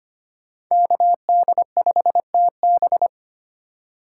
Morse code, heard as a keyed tone, about 25 wpm, 700 Hz. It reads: KD5TB
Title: USA Call Signs - 5 Characters - 6x Character Spacing - 25wpm